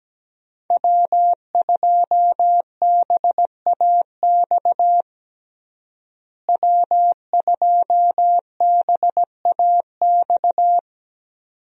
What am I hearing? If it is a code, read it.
Morse code, heard as a keyed tone, about 17 wpm, 700 Hz. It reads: W2BAX W2BAX
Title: USA Call Signs - 5 Characters - Repeated 2x - 17wpm